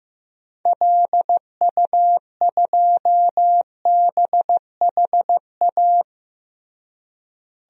Morse code, heard as a keyed tone, about 15 wpm, 700 Hz. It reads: LU2BHA
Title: DXCC-100 Call Signs - 6 Characters - Repeated 2x - 15wpm